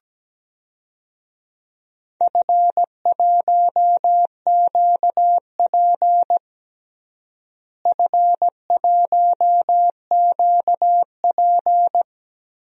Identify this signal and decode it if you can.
Morse code, heard as a keyed tone, about 17 wpm, 700 Hz. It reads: F1QP F1QP
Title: DXCC-100 Call Signs - 4 Characters - Repeated 2x - 17wpm